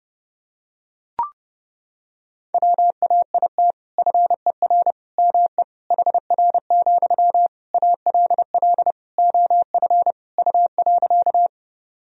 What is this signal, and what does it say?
Morse code, heard as a keyed tone, about 32 wpm, 700 Hz. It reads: WAIT FER ME HR, ALL OF U.
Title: Sentences from Top 500 Words and Abbreviations 30wpm